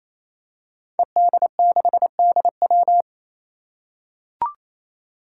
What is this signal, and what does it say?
Morse code, heard as a keyed tone, about 28 wpm, 700 Hz. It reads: ED6DW
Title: Five Challenging Letter-Number Combinations 28wpm